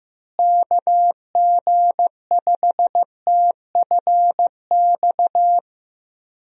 Morse code, heard as a keyed tone, about 15 wpm, 700 Hz. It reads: KG5TFX
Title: USA Call Signs - 6 Characters - Repeated 2x - 15wpm